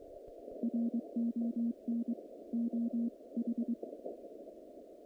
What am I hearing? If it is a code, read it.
Morse code, heard as a keyed tone, about 22 wpm, 230 Hz. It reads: RON OH
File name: contests / naqp-cw-k5tr-op-k5pi- -2200